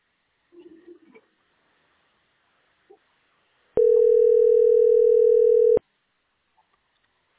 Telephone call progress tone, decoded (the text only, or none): ringback tone